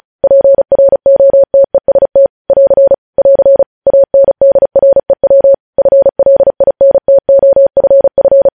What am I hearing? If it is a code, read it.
Morse code, heard as a keyed tone, about 36 wpm, 550 Hz. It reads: PROTEST <AR> <AR> ANDREW FLINTOFF
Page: news headlines in morse code at 35 wpm